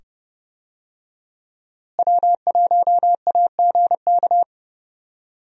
Morse code, heard as a keyed tone, about 30 wpm, 700 Hz. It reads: W1AGK
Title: USA Call Signs - 5 Characters - Repeated 2x - 30wpm